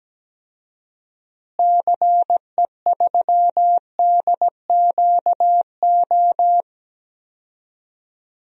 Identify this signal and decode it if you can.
Morse code, heard as a keyed tone, about 17 wpm, 700 Hz. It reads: CE3DQO
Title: DXCC-100 Call Signs - 6 Characters - Repeated 2x - 17wpm